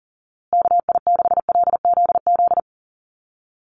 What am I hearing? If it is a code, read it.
Morse code, heard as a keyed tone, about 40 wpm, 700 Hz. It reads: KI6LZZ
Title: USA Call Signs Repeated 3x 40wpm